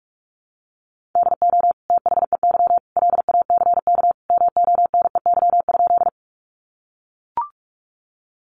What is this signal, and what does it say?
Morse code, heard as a keyed tone, about 44 wpm, 700 Hz. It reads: DO THEY LACK MONEY?
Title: Sentences from Top 1100 Words 45wpm